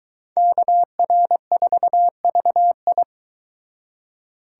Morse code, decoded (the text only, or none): KR4VI